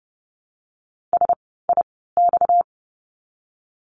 Morse code, heard as a keyed tone, about 30 wpm, spaced as for 17 wpm, 700 Hz. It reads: SIX
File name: course / 30:10wpm / 230 Lesson 37f - Letter x - Words 30wpm